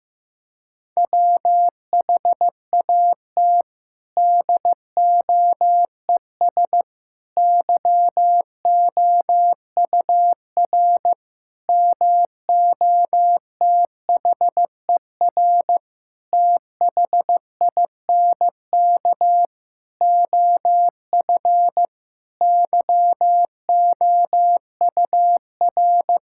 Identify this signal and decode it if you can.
Morse code, heard as a keyed tone, about 15 wpm, 700 Hz. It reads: WHAT DOES YOUR MOTHER THINK OF YOUR
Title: Sentences from Top 1000 Words - Review Part 5 - 15wpm